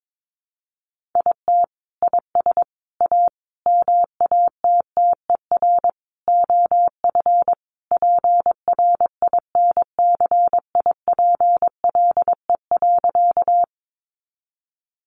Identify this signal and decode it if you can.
Morse code, heard as a keyed tone, about 22 wpm, 700 Hz. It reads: IT IS A MATTER OF PRINCIPLE.
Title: Sentences from Top 1400 Words 22wpm